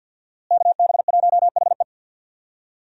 Morse code, heard as a keyed tone, about 50 wpm, 700 Hz. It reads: KB1HE
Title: USA Call Signs - 5 Characters - Repeated 3x - 50wpm